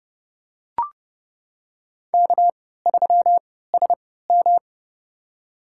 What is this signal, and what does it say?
Morse code, heard as a keyed tone, about 30 wpm, spaced as for 17 wpm, 700 Hz. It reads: K3SM
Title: USA Call Signs - 4 Characters - 3x Character Spacing - 30wpm